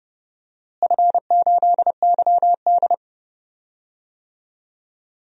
Morse code, heard as a keyed tone, about 32 wpm, 700 Hz. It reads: F8YD
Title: DXCC-100 Call Signs - 4 Characters - Repeated 2x - 30wpm